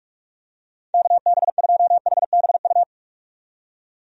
Morse code, heard as a keyed tone, about 46 wpm, 700 Hz. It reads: KB2HBU
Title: USA Call Signs - 6 Characters - 45wpm